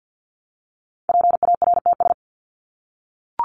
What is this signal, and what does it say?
Morse code, heard as a keyed tone, about 50 wpm, 700 Hz. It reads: PARTS
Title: ES:S Suffix - Top 100 Words with Base Words 50wpm